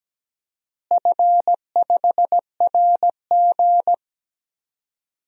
Morse code, heard as a keyed tone, about 17 wpm, 700 Hz. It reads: F5RG
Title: DXCC-100 Call Signs - 4 Characters - Repeated 3x - 17wpm